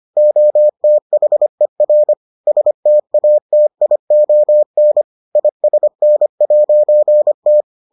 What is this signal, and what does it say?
Morse code, heard as a keyed tone, about 25 wpm, 600 Hz. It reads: OTHER STATION ISN'T